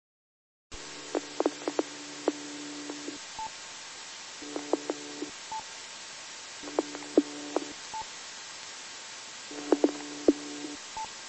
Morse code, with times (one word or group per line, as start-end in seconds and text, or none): none